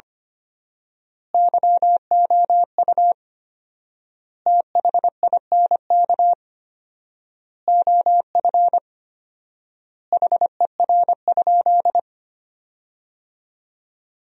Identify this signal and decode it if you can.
Morse code, heard as a keyed tone, about 25 wpm, 700 Hz. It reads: YOU THINK OF HER?
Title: 4x Word-Spacing - Sentences from Top 200 Words - 25wpm